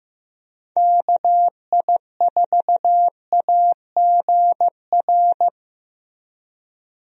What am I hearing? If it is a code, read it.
Morse code, heard as a keyed tone, about 15 wpm, 700 Hz. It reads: KI4AGR
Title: USA Call Signs - 6 Characters - Repeated 2x - 15wpm